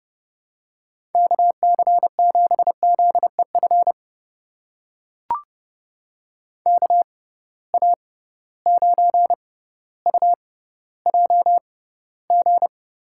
Morse code, heard as a keyed tone, about 30 wpm, spaced as for 15 wpm, 700 Hz. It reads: KC7ZEF KA9UJG
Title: USA Call Signs - 6 Characters - 6x Character Spacing - 30wpm